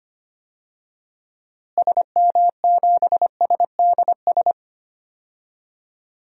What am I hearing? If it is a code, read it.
Morse code, heard as a keyed tone, about 25 wpm, 700 Hz. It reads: SM7SDS
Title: DXCC-100 Call Signs - 6 Characters - Repeated 3x - 25wpm